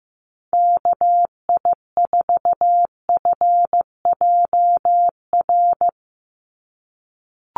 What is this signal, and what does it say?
Morse code, heard as a keyed tone, about 15 wpm, 700 Hz. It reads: KI4FJR
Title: USA Call Signs 15wpm